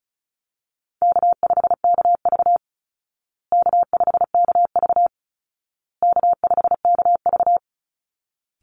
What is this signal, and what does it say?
Morse code, heard as a keyed tone, about 34 wpm, 700 Hz. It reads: K5KV K5KV K5KV
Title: Call Signs Repeated 2x - 35wpm